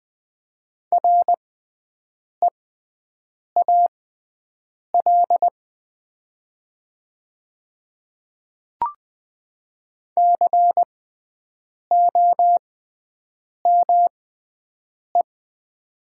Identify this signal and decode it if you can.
Morse code, heard as a keyed tone, about 20 wpm, spaced as for 7 wpm, 700 Hz. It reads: REAL COME
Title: 6x Character Spacing - Top 300 Words - 20wpm